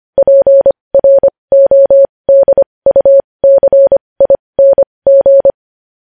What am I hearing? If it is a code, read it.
Morse code, heard as a keyed tone, about 25 wpm, spaced as for 20 wpm, 550 Hz. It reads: PRODUCING
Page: news headlines in morse code at 20 wpm